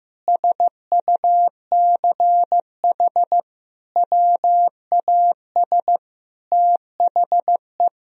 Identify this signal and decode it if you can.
Morse code, heard as a keyed tone, about 15 wpm, 700 Hz. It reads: SUCH WAS THE
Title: Sentences from Top 1000 Words - Review Part 3 - 15wpm